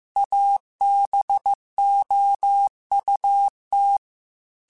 Morse code, heard as a keyed tone, about 15 wpm, 800 Hz. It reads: ABOUT
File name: QSOs / QSOs-mp3s / WD0FBT 1@13